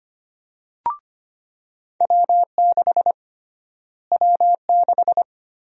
Morse code, heard as a keyed tone, about 25 wpm, 700 Hz. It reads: W6 W6
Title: USA Call Sign Prefixes - Repeated 2x - 25wpm